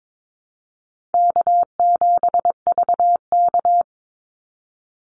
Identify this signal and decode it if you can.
Morse code, heard as a keyed tone, about 22 wpm, 700 Hz. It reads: K7VK
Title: NA SOTA Activator Callsigns 22wpm